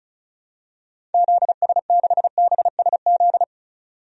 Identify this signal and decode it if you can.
Morse code, heard as a keyed tone, about 34 wpm, 700 Hz. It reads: ZS6BSZ
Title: DXCC-100 Call Signs - 6 Characters - 35wpm